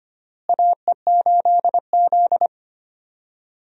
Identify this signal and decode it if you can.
Morse code, heard as a keyed tone, about 25 wpm, 700 Hz. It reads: AE8Z
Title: USA Call Signs - 4 Characters - Repeated 2x - 25wpm